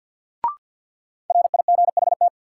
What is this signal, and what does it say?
Morse code, heard as a keyed tone, about 53 wpm, 700 Hz. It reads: RIGHT
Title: Sets of 2 Words - New Word Drill - 50wpm